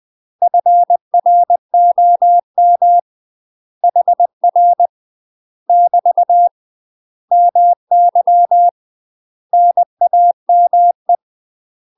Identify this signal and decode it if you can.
Morse code, heard as a keyed tone, about 20 wpm, 700 Hz. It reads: FROM HR <BT> MY NAME
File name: UK MASTER QSOs TRAINING / 20WPM / UK MASTER QSOs 0066